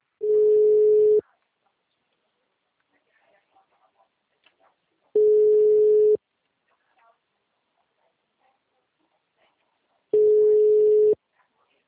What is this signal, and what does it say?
Ringback tone (Europe, 425 Hz).